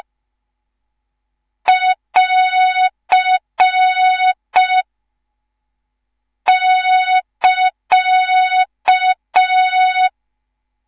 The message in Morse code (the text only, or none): <AR> <KA>